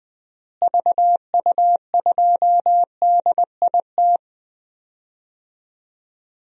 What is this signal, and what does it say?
Morse code, heard as a keyed tone, about 20 wpm, 700 Hz. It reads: VU2DIT
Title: DXCC-100 Call Signs - 6 Characters - Repeated 3x - 20wpm